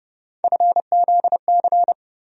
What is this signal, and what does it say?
Morse code, heard as a keyed tone, about 32 wpm, 700 Hz. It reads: FZC
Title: USA Call Sign Postfixes - 3 Letters 30wpm